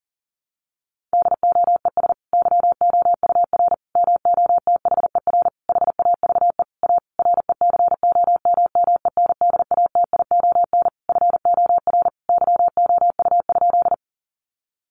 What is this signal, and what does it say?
Morse code, heard as a keyed tone, about 40 wpm, 700 Hz. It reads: DOES YOUR MOTHER HAVE A RECOMMENDATION FOR YOU?
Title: Sentences from Top 1400 Words 40wpm